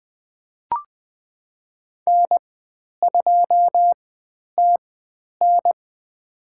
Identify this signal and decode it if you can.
Morse code, heard as a keyed tone, about 20 wpm, spaced as for 10 wpm, 700 Hz. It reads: N2TN
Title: 095 Lesson 16g - Letter c - Call Signs 20wpm